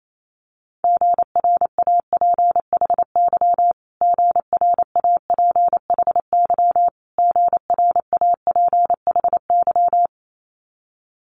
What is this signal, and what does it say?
Morse code, heard as a keyed tone, about 28 wpm, 700 Hz. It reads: GRAPHY GRAPHY GRAPHY
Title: Multi-letter Phonemes Repeated 3x 28wpm